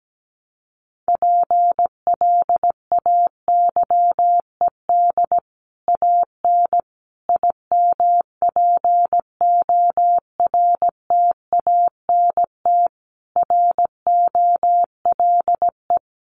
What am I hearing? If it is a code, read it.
Morse code, heard as a keyed tone, about 17 wpm, 700 Hz. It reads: PLAYED AN IMPORTANT ROLE
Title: Sets of 5 Words Encore2 17wpm